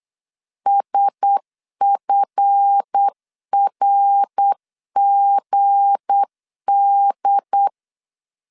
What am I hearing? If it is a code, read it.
Morse code, heard as a keyed tone, about 8 wpm, 800 Hz. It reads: SFRGD
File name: LEZIONI / 7 GF9Z / 7 GF 8wpm 3108